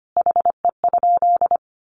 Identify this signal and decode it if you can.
Morse code, heard as a keyed tone, about 25 wpm, 700 Hz. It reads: HE?
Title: Sentences from Top 1300 Words 25wpm